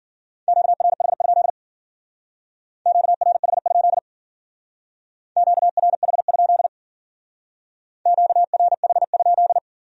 Morse code, heard as a keyed tone, about 50 wpm, 700 Hz. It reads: QRH? QRH? QRH? QRH?
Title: Speed-Racing Q-Codes - 40wpm